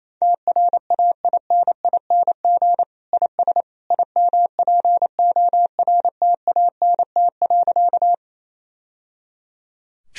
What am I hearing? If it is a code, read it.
Morse code, heard as a keyed tone, about 28 wpm, 700 Hz. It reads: TRAINING IS IMPORTANT.
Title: Sentences from Top 1000 Words - Review Part 4 - 28wpm